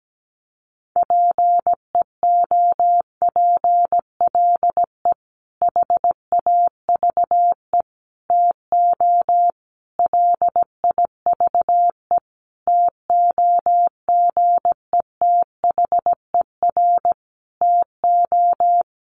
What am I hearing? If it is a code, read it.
Morse code, heard as a keyed tone, about 17 wpm, 700 Hz. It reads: PEOPLE HAVE TO LIVE TOGETHER TO
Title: Sentences from Top 1100 Words 17wpm